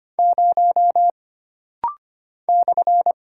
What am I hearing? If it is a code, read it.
Morse code, heard as a keyed tone, about 25 wpm, 700 Hz. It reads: 0 /
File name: Single Character (Letter:Number:Punctuation:Prosign) - Rapid-Fire - 25wpm